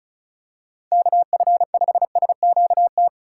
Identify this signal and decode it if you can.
Morse code, heard as a keyed tone, about 36 wpm, 700 Hz. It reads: KF5SQT
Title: USA Call Signs - 8x Character Spacing - 35wpm